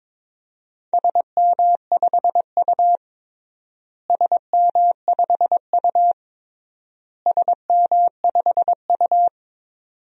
Morse code, heard as a keyed tone, about 22 wpm, 700 Hz. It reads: SM5U SM5U SM5U
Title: DXCC-100 Call Signs - 4 Characters - Repeated 3x - 22wpm